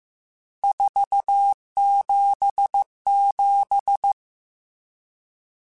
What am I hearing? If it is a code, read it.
Morse code, heard as a keyed tone, about 15 wpm, 800 Hz. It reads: 477
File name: RSTs / RST-mp3s / Various RST @13